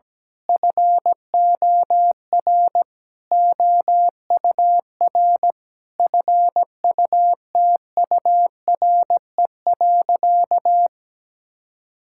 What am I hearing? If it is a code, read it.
Morse code, heard as a keyed tone, about 17 wpm, 700 Hz. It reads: FOR OUR FUTURE.
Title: Sentences from Top 1000 Words - Review Part 6 - 17wpm